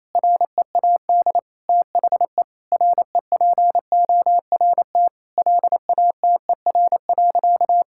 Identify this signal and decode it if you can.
Morse code, heard as a keyed tone, about 28 wpm, 700 Hz. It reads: READ THE REPORT LATER.
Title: Sentences from Top 1000 Words - Review Part 6 - 28wpm